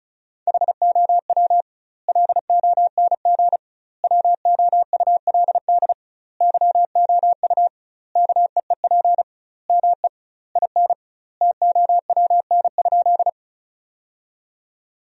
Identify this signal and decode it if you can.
Morse code, heard as a keyed tone, about 36 wpm, 700 Hz. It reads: HOW LONG WOULD YOU KEEP ME IN TOWN?
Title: 2x Word-Spacing - Sentences from Top 500 Words - 35wpm